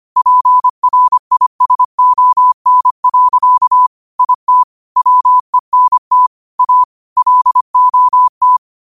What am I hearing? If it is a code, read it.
Morse code, heard as a keyed tone, about 25 wpm, 1000 Hz. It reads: PRISON. IT WENT A LOT
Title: 25 WPM morse code quotes for Thu, 29 Feb 2024